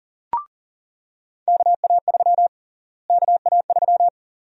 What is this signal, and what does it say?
Morse code, heard as a keyed tone, about 40 wpm, 700 Hz. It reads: KA3 KA3
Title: USA Call Sign Prefixes - Repeated 2x - 40wpm